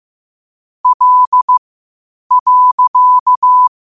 Morse code, heard as a keyed tone, about 15 wpm, spaced as for 8 wpm, 1000 Hz. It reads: L.
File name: qotdfiles / 8wpm